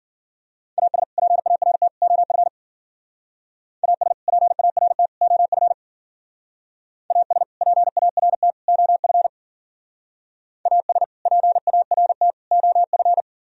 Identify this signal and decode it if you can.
Morse code, heard as a keyed tone, about 53 wpm, 700 Hz. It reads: AS PART OF AS PART OF AS PART OF AS PART OF